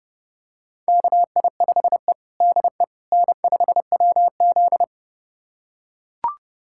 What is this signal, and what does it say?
Morse code, heard as a keyed tone, about 30 wpm, 700 Hz. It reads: KI5E DE N5WZ
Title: USA Callsign DE Callsign - 4 Characters - 30wpm